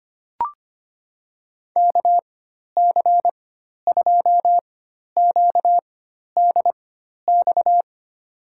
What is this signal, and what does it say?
Morse code, heard as a keyed tone, about 25 wpm, spaced as for 12 wpm, 700 Hz. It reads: KC2QDX
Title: USA Call Signs - 6 Characters - 4x Character Spacing - 25wpm